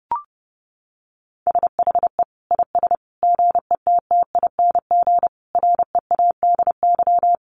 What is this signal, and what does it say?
Morse code, heard as a keyed tone, about 30 wpm, 700 Hz. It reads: SHE IS GETTING READY
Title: Easy Phrases 30wpm